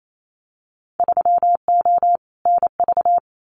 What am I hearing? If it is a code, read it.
Morse code, heard as a keyed tone, about 28 wpm, 700 Hz. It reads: 3O NV
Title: Winter Field Day Exchanges Encore 28wpm